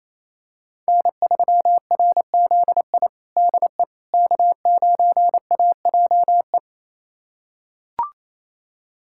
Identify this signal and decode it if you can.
Morse code, heard as a keyed tone, about 28 wpm, 700 Hz. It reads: N3RZI DE K9AJE
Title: USA Callsign DE Callsign - 5 Characters - 28wpm